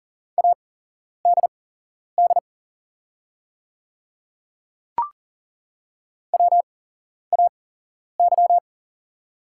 Morse code, heard as a keyed tone, about 40 wpm, spaced as for 11 wpm, 700 Hz. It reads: ADD WAY
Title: 8x Character Spacing - Top 200 Words - 40wpm